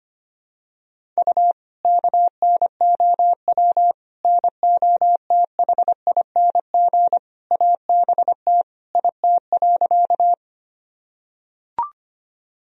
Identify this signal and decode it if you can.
Morse code, heard as a keyed tone, about 25 wpm, 700 Hz. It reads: U KNOW NOTHING ABT IT.
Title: Sentences from Top 500 Words and Abbreviations 25wpm